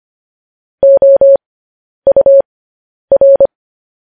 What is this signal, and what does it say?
Morse code, heard as a keyed tone, about 25 wpm, spaced as for 10 wpm, 550 Hz. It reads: OUR
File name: podcast / News headlines-1x-25x10